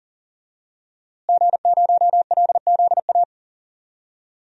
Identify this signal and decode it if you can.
Morse code, heard as a keyed tone, about 40 wpm, 700 Hz. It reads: G0LZA
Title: DXCC-100 Call Signs - 5 Characters - Repeated 2x - 40wpm